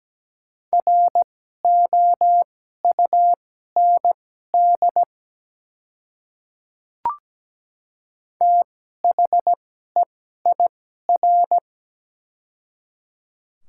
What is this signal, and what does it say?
Morse code, heard as a keyed tone, about 17 wpm, spaced as for 12 wpm, 690 Hz. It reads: ROUND THEIR
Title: 2x Character Spacing - Top 300 Words - 17wpm